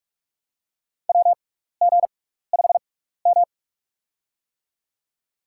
Morse code, heard as a keyed tone, about 42 wpm, spaced as for 15 wpm, 700 Hz. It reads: WG5M
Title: USA Call Signs - 4 Characters - 6x Character Spacing - 45wpm